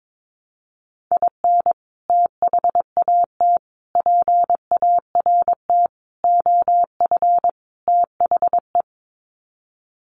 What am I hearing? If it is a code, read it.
Morse code, heard as a keyed tone, about 22 wpm, 700 Hz. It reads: IN THAT PART OF THE
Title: Sets of 5 Words Encore2 22wpm